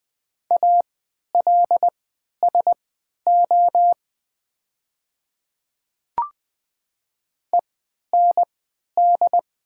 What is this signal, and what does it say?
Morse code, heard as a keyed tone, about 20 wpm, spaced as for 11 wpm, 690 Hz. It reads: ALSO END
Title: 3x Character Spacing - Top 200 Words - 20wpm